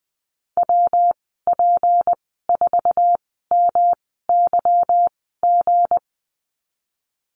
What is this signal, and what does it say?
Morse code, heard as a keyed tone, about 20 wpm, spaced as for 15 wpm, 700 Hz. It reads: WP4MYG
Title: 193 Lesson 31g - Letter v - Call Signs 20wpm